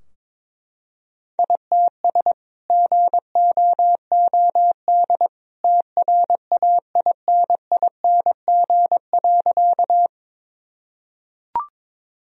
Morse code, heard as a keyed tone, about 22 wpm, 700 Hz. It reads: ITS GOOD TRAINING.
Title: Sentences from Top 1000 Words - Review Part 4 - 22wpm